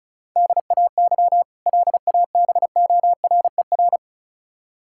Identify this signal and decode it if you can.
Morse code, heard as a keyed tone, about 36 wpm, 700 Hz. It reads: DAY LABORER
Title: K1USN SST QSO Practice 35wpm